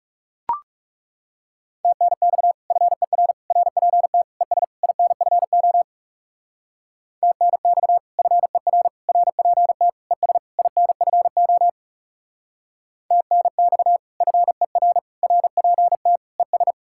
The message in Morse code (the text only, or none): TNX FER RPT ES INFO TNX FER RPT ES INFO TNX FER RPT ES